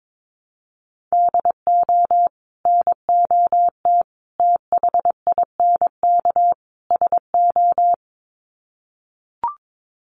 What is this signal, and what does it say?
Morse code, heard as a keyed tone, about 22 wpm, 700 Hz. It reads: DO NOT THINK SO
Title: Sets of 4 Words 22wpm